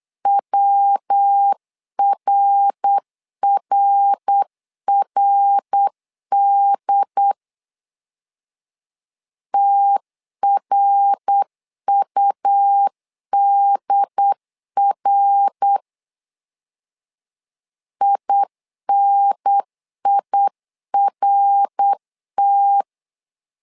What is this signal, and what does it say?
Morse code, heard as a keyed tone, about 8 wpm, 800 Hz. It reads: WRRRD TRUDR INIRT
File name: LEZIONI / 3 MWRO / 3 MWR 8wpm 2444